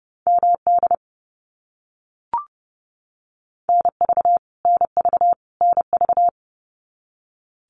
Morse code, heard as a keyed tone, about 30 wpm, 700 Hz. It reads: MD NV NV NV
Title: US States Repeated 3x 30wpm